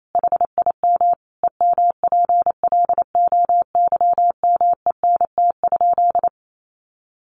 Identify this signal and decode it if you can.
Morse code, heard as a keyed tone, about 28 wpm, 700 Hz. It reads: HIM EMPLOYMENT?
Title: Sentences from Top 1500 Words 28wpm